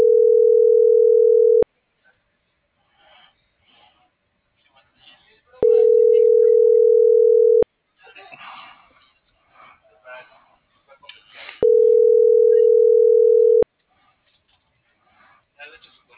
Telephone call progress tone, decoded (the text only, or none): ringback tone